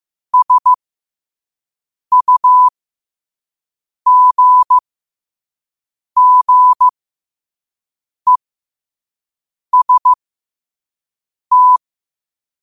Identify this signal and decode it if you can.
Morse code, heard as a keyed tone, about 15 wpm, spaced as for 5 wpm, 1000 Hz. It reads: SUGGEST
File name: qotdfiles / 5wpm